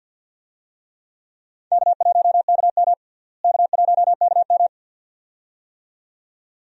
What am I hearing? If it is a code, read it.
Morse code, heard as a keyed tone, about 51 wpm, 700 Hz. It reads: K1KM K1KM
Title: USA Call Signs - 4 Characters - Repeated 2x - 50wpm